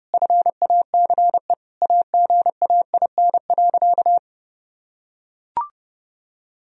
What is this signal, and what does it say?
Morse code, heard as a keyed tone, about 30 wpm, 700 Hz. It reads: FACE AGAIN.
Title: Sentences from Top 1000 Words - Review Part 2 - 30wpm